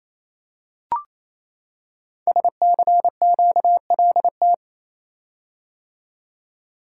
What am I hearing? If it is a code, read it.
Morse code, heard as a keyed tone, about 28 wpm, 700 Hz. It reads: SCQLT
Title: Code Groups - 5 Letters - 28wpm